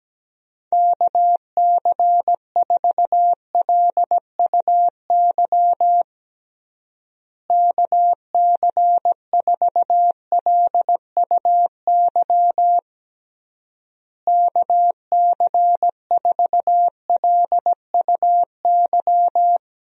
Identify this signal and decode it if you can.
Morse code, heard as a keyed tone, about 17 wpm, 700 Hz. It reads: KC4LUY KC4LUY KC4LUY
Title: USA Call Signs - 6 Characters - Repeated 3x - 17wpm